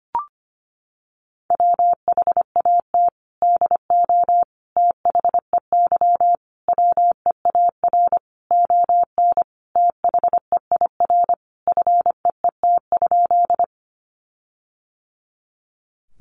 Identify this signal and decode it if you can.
Morse code, heard as a keyed tone, about 25 wpm, 700 Hz. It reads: WHAT DO THEY WEAR ON THEIR FEET?
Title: Sentences from Top 1500 Words 25wpm